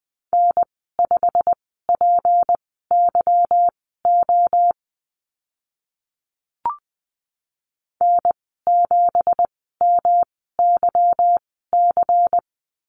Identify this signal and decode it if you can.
Morse code, heard as a keyed tone, about 20 wpm, spaced as for 15 wpm, 700 Hz. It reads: N5PYO N7MYC
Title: 167 Lesson 27g - Number 7 - Call Signs 20wpm